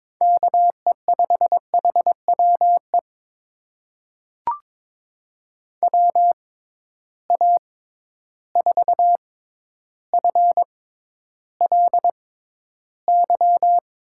Morse code, heard as a keyed tone, about 22 wpm, spaced as for 11 wpm, 700 Hz. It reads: KE5HWE WA4FLY
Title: USA Call Signs - 6 Characters - 6x Character Spacing - 22wpm